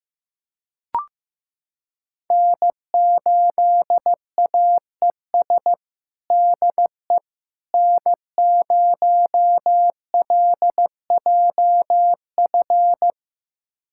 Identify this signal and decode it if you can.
Morse code, heard as a keyed tone, about 15 wpm, 700 Hz. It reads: N8AES DE N0LJF
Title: USA Callsign DE Callsign - 5 Characters - 15wpm